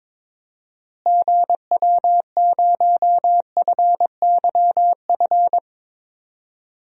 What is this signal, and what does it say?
Morse code, heard as a keyed tone, about 22 wpm, 700 Hz. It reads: GW0FYF
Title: DXCC-100 Call Signs - 6 Characters - Repeated 3x - 22wpm